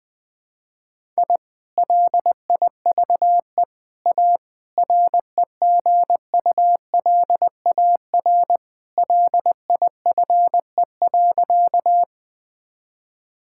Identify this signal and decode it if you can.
Morse code, heard as a keyed tone, about 20 wpm, 700 Hz. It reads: I LIVE A REGULAR LIFE.
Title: Sentences from Top 1000 Words - Review Part 6 - 20wpm